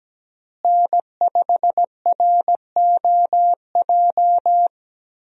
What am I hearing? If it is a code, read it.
Morse code, heard as a keyed tone, about 17 wpm, 700 Hz. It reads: N5ROJ
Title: USA Call Signs - 5 Characters - Repeated 2x - 17wpm